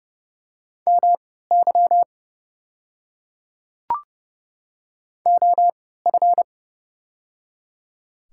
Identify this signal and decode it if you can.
Morse code, heard as a keyed tone, about 30 wpm, spaced as for 17 wpm, 700 Hz. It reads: MY OF
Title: 3x Character Spacing - Top 100 Words - 30wpm